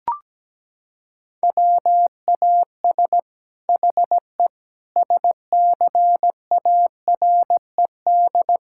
Morse code, heard as a keyed tone, about 17 wpm, 700 Hz. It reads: WAS HE SCARED
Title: Sentences from Top 1000 Words - Review Part 6 - 17wpm